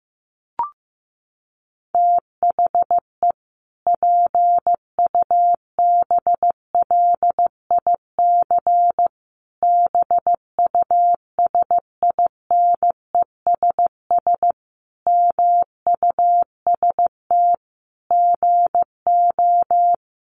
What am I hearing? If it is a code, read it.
Morse code, heard as a keyed tone, about 15 wpm, 700 Hz. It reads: THE PUBLIC BUSINESS MUST GO
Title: Sentences from Top 1300 Words 15wpm